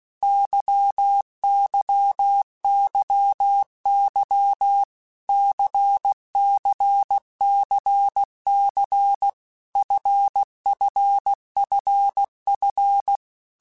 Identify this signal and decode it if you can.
Morse code, heard as a keyed tone, about 16 wpm, 800 Hz. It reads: YYYY CCCC FFFF
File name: Basic Letter Groups By7Groups / LFCYQ / LFCYQ@15